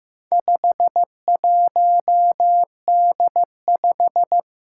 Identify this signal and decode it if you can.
Morse code, heard as a keyed tone, about 15 wpm, 700 Hz. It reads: 51D5
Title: Four Challenging Letter-Number Combinations 15wpm